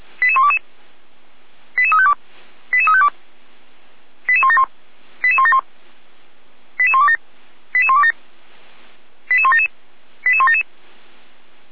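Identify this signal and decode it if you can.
Five-tone selective call (ZVEI), heard as quick runs of five tones, tones of about 70 ms. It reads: 88122 88352 88352 88171 88171 88127 88127 88177 88177
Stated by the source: ZVEI five-tone selective call 88200, received on an SDR.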